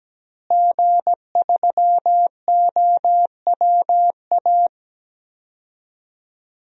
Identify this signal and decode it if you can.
Morse code, heard as a keyed tone, about 17 wpm, 700 Hz. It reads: G3OWA
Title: DXCC-100 Call Signs - 5 Characters - Repeated 3x - 17wpm